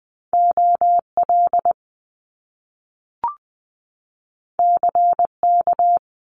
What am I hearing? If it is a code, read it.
Morse code, heard as a keyed tone, about 19 wpm, 700 Hz. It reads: OL CK
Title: Top Two Letter Combinations in Words 20wpm